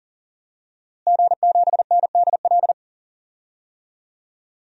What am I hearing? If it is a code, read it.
Morse code, heard as a keyed tone, about 40 wpm, 700 Hz. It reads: G7NDL
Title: DXCC-100 Call Signs - 5 Characters - Repeated 3x - 40wpm